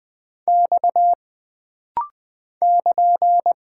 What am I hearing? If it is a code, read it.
Morse code, heard as a keyed tone, about 20 wpm, 700 Hz. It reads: X <KN>
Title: Single Character (Letter:Number:Punctuation:Prosign) - Rapid-Fire - 20wpm